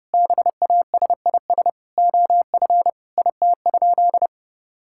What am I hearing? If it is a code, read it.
Morse code, heard as a keyed tone, about 30 wpm, 700 Hz. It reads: BASIS OF IT?
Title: Sentences from Top 1000 Words - Review Part 5 - 30wpm